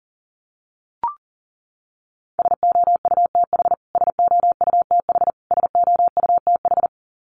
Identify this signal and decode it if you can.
Morse code, heard as a keyed tone, about 40 wpm, 700 Hz. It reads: SOUTH SOUTH SOUTH
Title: Top 500 Words Repeated 40wpm